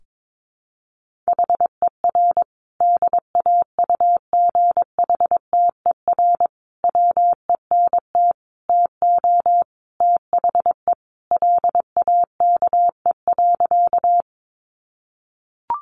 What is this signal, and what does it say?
Morse code, heard as a keyed tone, about 22 wpm, 700 Hz. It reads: HER DAUGHTER WENT TO THE LAKE.
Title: Sentences from Top 1500 Words 22wpm